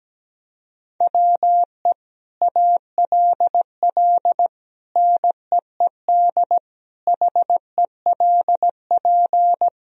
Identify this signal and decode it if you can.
Morse code, heard as a keyed tone, about 17 wpm, 700 Hz. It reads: WE ALL NEED HELP
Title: Sentences from Top 1000 Words - Review Part 3 - 17wpm